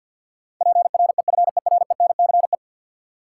Morse code, heard as a keyed tone, about 53 wpm, 700 Hz. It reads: PREFERENCE
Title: Sentences from Top 1500 Words - New Word Drill - 50wpm